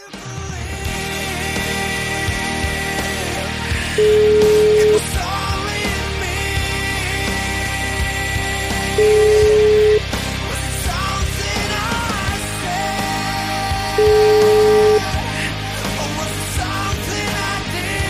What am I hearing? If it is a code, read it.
Ringback tone (Europe, 425 Hz), with noise.